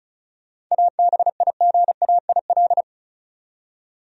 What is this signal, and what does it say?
Morse code, heard as a keyed tone, about 34 wpm, 700 Hz. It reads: ABIGAIL
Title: USA Common Names - 35wpm